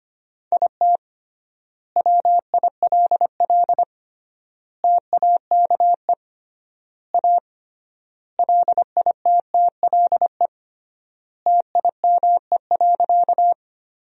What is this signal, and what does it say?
Morse code, heard as a keyed tone, about 25 wpm, 700 Hz. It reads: IT WILL TAKE A LITTLE TIME.
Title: 3x Word-Spacing - Sentences from Top 200 Words - 25wpm